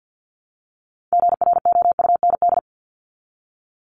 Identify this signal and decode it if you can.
Morse code, heard as a keyed tone, about 53 wpm, 700 Hz. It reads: GROUND
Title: Top 500 Words 50wpm